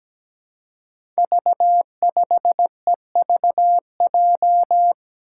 Morse code, heard as a keyed tone, about 17 wpm, 700 Hz. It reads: V5EVJ
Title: Five Challenging Letter-Number Combinations 17wpm